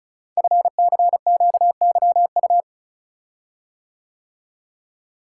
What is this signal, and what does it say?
Morse code, heard as a keyed tone, about 36 wpm, 700 Hz. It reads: FCQYU
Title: Code Groups - 5 Letters - 35wpm